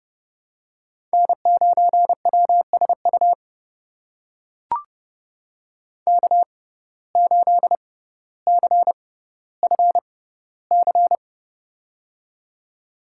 Morse code, heard as a keyed tone, about 32 wpm, spaced as for 16 wpm, 700 Hz. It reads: N9WSU K8CFC
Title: USA Call Signs - 5 Characters - 6x Character Spacing - 30wpm